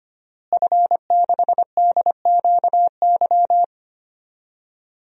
Morse code, heard as a keyed tone, about 25 wpm, 700 Hz. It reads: F6DQY
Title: DXCC-100 Call Signs - 5 Characters - Repeated 3x - 25wpm